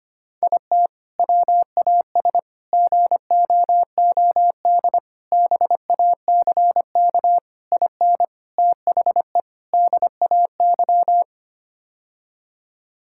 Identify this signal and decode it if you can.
Morse code, heard as a keyed tone, about 25 wpm, 700 Hz. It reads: IT WAS GOOD BACK IN THE DAY
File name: practice / 25wpm / Sentences with 3-Word Phrases - 25wpm